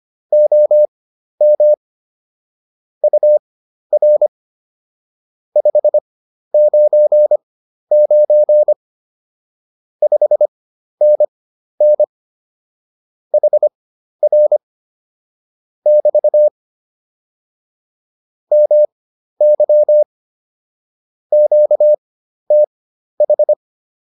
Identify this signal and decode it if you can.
Morse code, heard as a keyed tone, about 25 wpm, spaced as for 12 wpm, 600 Hz. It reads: OM UR 599 5NN HR <BT> MY QTH